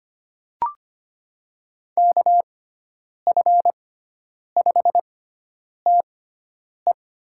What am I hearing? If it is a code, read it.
Morse code, heard as a keyed tone, about 25 wpm, spaced as for 9 wpm, 700 Hz. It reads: KF5TE
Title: USA Call Signs - 6 Characters - 6x Character Spacing - 25wpm